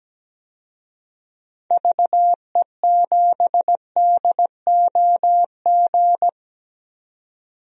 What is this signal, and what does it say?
Morse code, heard as a keyed tone, about 17 wpm, 700 Hz. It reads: VE7DOG
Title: DXCC-100 Call Signs - 6 Characters - 17wpm